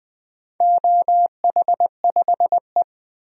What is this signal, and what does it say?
Morse code, heard as a keyed tone, about 20 wpm, 700 Hz. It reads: OH5E
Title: DXCC-100 Call Signs - 4 Characters - 20wpm